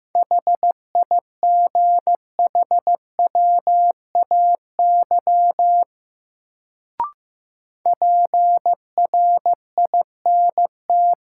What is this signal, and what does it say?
Morse code, heard as a keyed tone, about 15 wpm, 700 Hz. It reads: HIGHWAY PRINT
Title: Sentences from Top 1400 Words - New Word Drill Alt - 15wpm